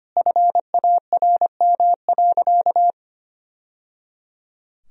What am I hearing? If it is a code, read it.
Morse code, heard as a keyed tone, about 25 wpm, 700 Hz. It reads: FARM.
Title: Sentences from Top 1000 Words - Review Part 2 - 25wpm